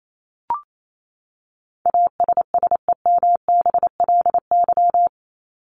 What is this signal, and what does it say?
Morse code, heard as a keyed tone, about 28 wpm, 700 Hz. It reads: ASSEMBLY
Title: LY Suffix - Top 100 Words with Base Words 28wpm